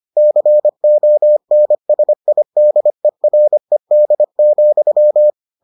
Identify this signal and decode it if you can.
Morse code, heard as a keyed tone, about 25 wpm, 600 Hz. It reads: CONSIDERED,